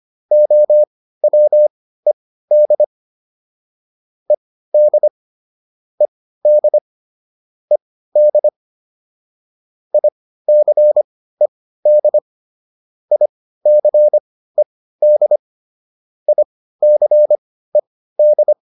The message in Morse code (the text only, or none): OWED ED ED ED ICED ICED ICED